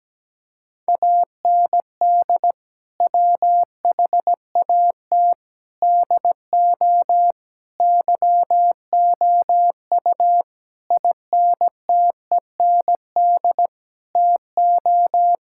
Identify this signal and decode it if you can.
Morse code, heard as a keyed tone, about 17 wpm, 700 Hz. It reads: AND WHAT DO YOU INTEND TO
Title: Sentences from Top 1600 Words 17wpm